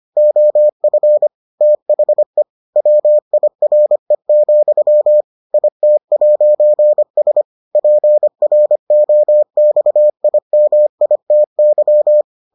Morse code, heard as a keyed tone, about 25 wpm, 600 Hz. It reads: OF THE WIRE, IT'S PROXIMITY